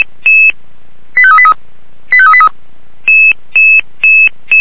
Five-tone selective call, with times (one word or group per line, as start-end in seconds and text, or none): 1.2-1.5 85383
2.1-2.5 85383